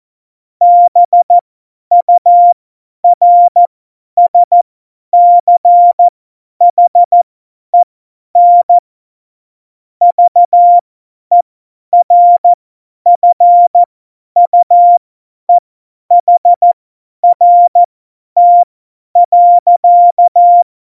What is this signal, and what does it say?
Morse code, heard as a keyed tone, about 14 wpm, spaced as for 10 wpm, 700 Hz. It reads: BURSCHEN VERFUEHRT.
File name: Morsekurs / Witze / Witz 20. Oktober 2020, 50 F70